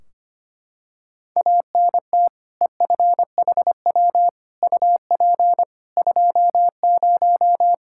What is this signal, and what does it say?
Morse code, heard as a keyed tone, about 25 wpm, 700 Hz. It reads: ANT EFHW UP 20
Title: Practice QSO - Exchange 2 - Info - 25wpm